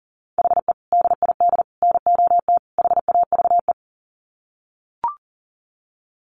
Morse code, heard as a keyed tone, about 40 wpm, 700 Hz. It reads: HE DID NOT HAVE
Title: Sets of 4 Words 40wpm